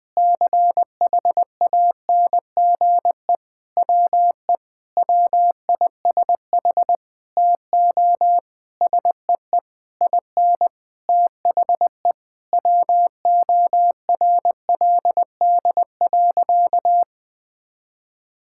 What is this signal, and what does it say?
Morse code, heard as a keyed tone, about 21 wpm, 700 Hz. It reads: CHANGE WE WISH TO SEE IN THE WORLD.
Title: Sentences with 9-Word Phrases - 20wpm